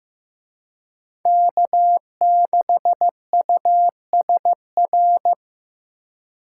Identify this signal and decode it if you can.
Morse code, heard as a keyed tone, about 18 wpm, 700 Hz. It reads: K6USR
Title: Speed Racing - USA 1x3 Call Signs 15wpm